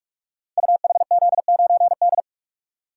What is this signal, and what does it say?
Morse code, heard as a keyed tone, about 44 wpm, 700 Hz. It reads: UHZ9D
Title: Code Groups - 5 Letter:Numbers - 45wpm